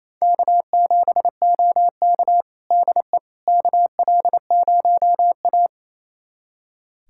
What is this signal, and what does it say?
Morse code, heard as a keyed tone, about 28 wpm, 700 Hz. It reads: K7OK DE KL0A
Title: USA Callsign DE Callsign - 4 Characters - 28wpm